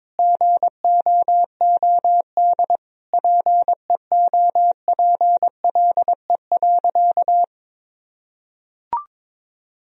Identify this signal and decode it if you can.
Morse code, heard as a keyed tone, about 22 wpm, 700 Hz. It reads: GOOD PEOPLE.
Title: Sentences from Top 1000 Words - Review Part 2 - 22wpm